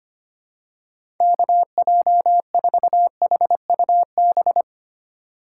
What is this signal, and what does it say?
Morse code, heard as a keyed tone, about 25 wpm, 700 Hz. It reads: KJ4HUB
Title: USA Call Signs - 6 Characters - 4x Character Spacing - 25wpm